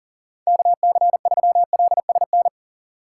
Morse code, heard as a keyed tone, about 51 wpm, 700 Hz. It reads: KC3LSN